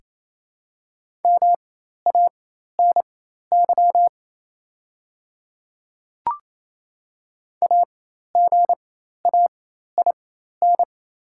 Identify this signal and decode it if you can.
Morse code, heard as a keyed tone, about 27 wpm, spaced as for 13 wpm, 700 Hz. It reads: MANY AGAIN
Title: 4x Character Spacing - Top 200 Words - 28wpm